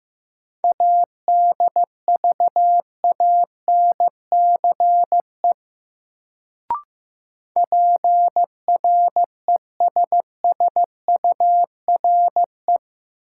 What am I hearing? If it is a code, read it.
Morse code, heard as a keyed tone, about 15 wpm, 700 Hz. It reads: ADVANCE PRESSURE
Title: Sentences from Top 1400 Words - New Word Drill Alt - 15wpm